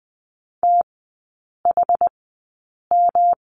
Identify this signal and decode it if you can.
Morse code, heard as a keyed tone, about 20 wpm, spaced as for 8 wpm, 700 Hz. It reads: THM
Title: 107 Lesson 18e - Letter m - Three Characters 20wpm